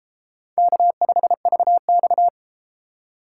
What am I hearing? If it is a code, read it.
Morse code, heard as a keyed tone, about 32 wpm, 700 Hz. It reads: K5VX